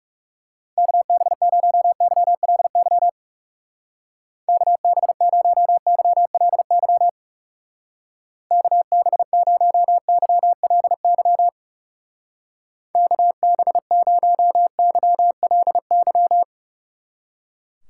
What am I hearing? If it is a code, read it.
Morse code, heard as a keyed tone, about 39 wpm, 700 Hz. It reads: KB0YLY KB0YLY KB0YLY KB0YLY